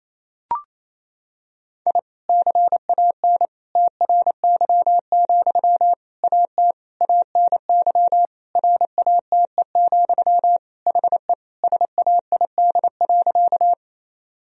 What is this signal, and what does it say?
Morse code, heard as a keyed tone, about 27 wpm, 700 Hz. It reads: I CAN TRY, AT ANY RATE, HE SAID.
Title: Sentences from Top 1000 Words - Review Part 4 - 28wpm